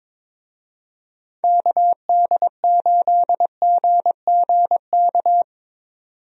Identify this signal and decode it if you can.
Morse code, heard as a keyed tone, about 22 wpm, 700 Hz. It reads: KD8GGK
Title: USA Call Signs - 6 Characters - 22wpm